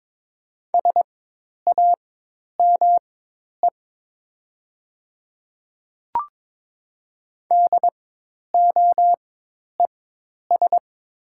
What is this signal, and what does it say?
Morse code, heard as a keyed tone, about 22 wpm, spaced as for 10 wpm, 700 Hz. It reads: SAME DOES
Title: 4x Character Spacing - Top 200 Words - 22wpm